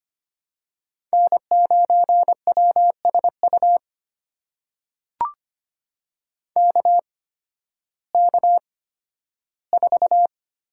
Morse code, heard as a keyed tone, about 25 wpm, spaced as for 13 wpm, 700 Hz. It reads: N9WSU KK4
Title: USA Call Signs - 8x Character Spacing - 25wpm